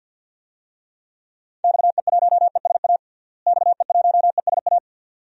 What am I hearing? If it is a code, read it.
Morse code, heard as a keyed tone, about 52 wpm, 700 Hz. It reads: XE1ESA XE1ESA
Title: DXCC-100 Call Signs - 6 Characters - Repeated 2x - 50wpm